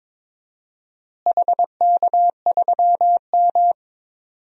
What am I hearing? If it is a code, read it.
Morse code, heard as a keyed tone, about 22 wpm, 700 Hz. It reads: HK3M
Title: DXCC-100 Call Signs - 4 Characters - Repeated 3x - 22wpm